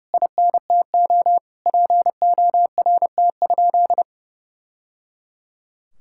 Morse code, heard as a keyed tone, about 30 wpm, 700 Hz. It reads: INTO PORT?
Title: Sentences from Top 1000 Words - Review Part 5 - 30wpm